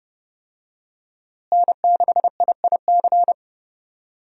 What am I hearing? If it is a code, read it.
Morse code, heard as a keyed tone, about 32 wpm, 700 Hz. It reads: N6IIC
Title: USA Call Signs - 5 Characters - Repeated 3x - 30wpm